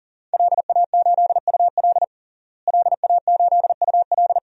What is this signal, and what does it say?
Morse code, heard as a keyed tone, about 37 wpm, 700 Hz. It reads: LA8UL LA8UL
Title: DXCC-100 Call Signs - 5 Characters - Repeated 2x - 40wpm